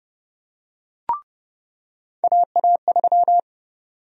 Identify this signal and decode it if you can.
Morse code, heard as a keyed tone, about 30 wpm, 700 Hz. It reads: AA3
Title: USA Call Sign Prefixes - 30wpm